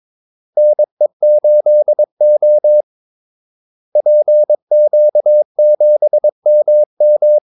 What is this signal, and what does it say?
Morse code, heard as a keyed tone, about 22 wpm, 600 Hz. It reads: NE8O PQ7MM
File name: Callsigns-22wpm